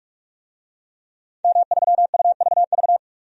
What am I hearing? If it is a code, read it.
Morse code, heard as a keyed tone, about 46 wpm, 700 Hz. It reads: M3UVV
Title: DXCC-100 Call Signs - 5 Characters - Repeated 3x - 45wpm